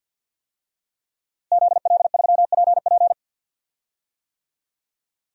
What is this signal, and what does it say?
Morse code, heard as a keyed tone, about 49 wpm, 700 Hz. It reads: ZL3PP
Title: DXCC-100 Call Signs - 5 Characters - Repeated 2x - 50wpm